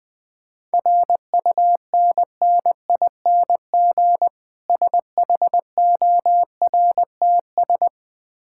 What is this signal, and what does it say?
Morse code, heard as a keyed tone, about 20 wpm, 690 Hz. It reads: RUNNING SHORTS
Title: Difficult Words with Hints - 20wpm